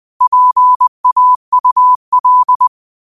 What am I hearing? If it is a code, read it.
Morse code, heard as a keyed tone, about 20 wpm, 1000 Hz. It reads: PAUL